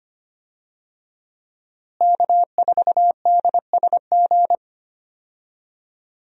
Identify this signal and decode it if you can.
Morse code, heard as a keyed tone, about 25 wpm, 700 Hz. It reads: K4DSG
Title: USA Call Signs - 5 Characters - Repeated 2x - 25wpm